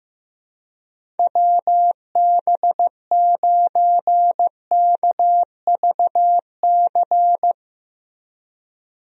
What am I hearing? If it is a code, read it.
Morse code, heard as a keyed tone, about 15 wpm, 700 Hz. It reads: WB9KVC
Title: USA Call Signs - 6 Characters - Repeated 3x - 15wpm